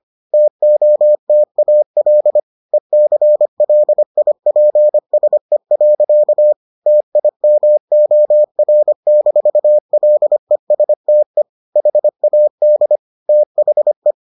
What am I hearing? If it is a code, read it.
Morse code, heard as a keyed tone, about 25 wpm, 600 Hz. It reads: TOTAL ECLIPSE. TIMOR-LESTE HAD THE